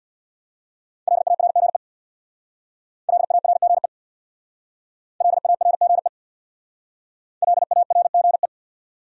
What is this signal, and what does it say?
Morse code, heard as a keyed tone, about 62 wpm, 700 Hz. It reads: LARGE LARGE LARGE LARGE
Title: Speed-Racing Top 500 Words 50wpm